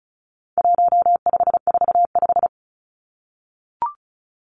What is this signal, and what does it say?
Morse code, heard as a keyed tone, about 34 wpm, 700 Hz. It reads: 1545
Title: Numbers - 4-Digit 35wpm